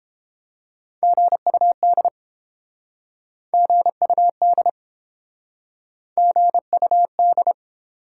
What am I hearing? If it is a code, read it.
Morse code, heard as a keyed tone, about 30 wpm, 700 Hz. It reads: GUD GUD GUD